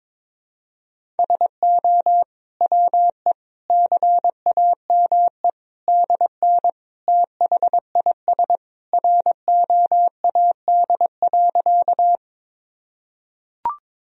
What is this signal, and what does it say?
Morse code, heard as a keyed tone, about 21 wpm, 700 Hz. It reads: SO WE CAME DN THIS ROAD.
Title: Sentences from Top 400 Words and Abbreviations 22wpm